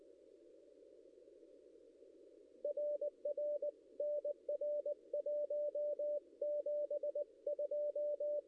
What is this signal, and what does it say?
Morse code, heard as a keyed tone, about 20 wpm, 590 Hz. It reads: RR NR 172
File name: contests / ss-cw-k5tr-op-n5rz -1700